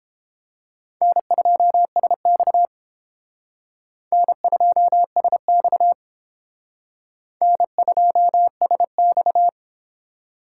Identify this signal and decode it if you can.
Morse code, heard as a keyed tone, about 31 wpm, 700 Hz. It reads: N2SX N2SX N2SX